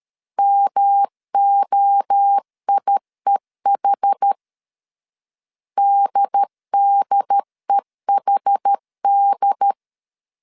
Morse code, heard as a keyed tone, about 13 wpm, 800 Hz. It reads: MOIEH DDEHD
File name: CW / DET / 1 3 1500C S C 12wpm 2750